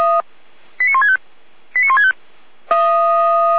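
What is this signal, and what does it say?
Five-tone selective call (ZVEI), heard as quick runs of five tones, tones of about 70 ms. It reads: 89165 89165
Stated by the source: Five-tone ZVEI selective call received by SDR, archived by Feuerwehr Orlamünde.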